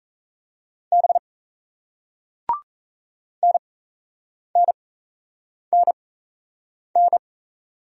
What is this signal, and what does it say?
Morse code, heard as a keyed tone, about 38 wpm, spaced as for 8 wpm, 700 Hz. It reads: B NNNN